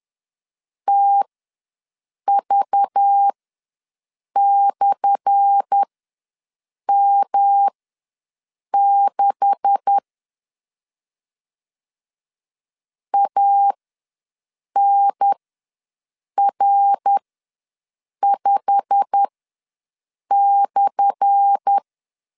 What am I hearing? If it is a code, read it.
Morse code, heard as a keyed tone, about 10 wpm, spaced as for 6 wpm, 800 Hz. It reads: TV/M6 ANR5/
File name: LEZIONI / 10 SIGNS / 10A SLASH ws 10wpm 4121